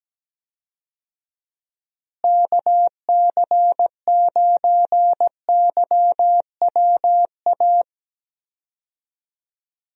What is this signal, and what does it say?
Morse code, heard as a keyed tone, about 17 wpm, 700 Hz. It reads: KC9YWA
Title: USA Call Signs - 6 Characters - Repeated 2x - 17wpm